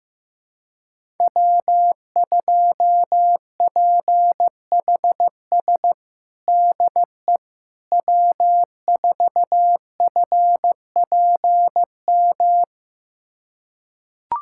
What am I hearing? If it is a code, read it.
Morse code, heard as a keyed tone, about 15 wpm, 700 Hz. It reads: W2PHS DE W4FPM
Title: USA Callsign DE Callsign - 5 Characters - 15wpm